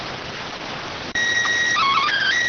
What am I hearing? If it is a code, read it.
SSTV (slow-scan television), PD 290.